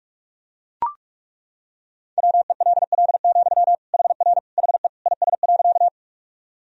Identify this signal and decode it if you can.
Morse code, heard as a keyed tone, about 41 wpm, 700 Hz. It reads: WELL, HR HE IS.
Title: Sentences from Top 200 Words and Abbreviations 45wpm